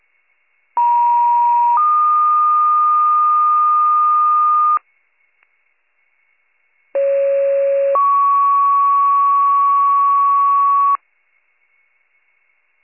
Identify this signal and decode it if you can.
Two-tone page, heard as a one-second tone, then a three-second tone, about 950 then 1220 Hz, about 550 then 1120 Hz. It is heard from 0.8 to 11.0 s.